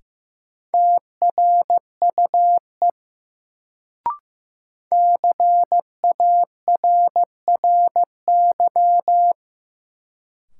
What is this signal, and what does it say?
Morse code, heard as a keyed tone, about 15 wpm, 700 Hz. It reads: TRUE CARRY
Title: Sentences from Top 400 Words - New Word Drill - 15wpm